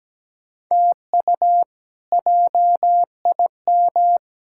Morse code, heard as a keyed tone, about 17 wpm, 700 Hz. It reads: TU JIM
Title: K1USN SST QSO Practice - 17wpm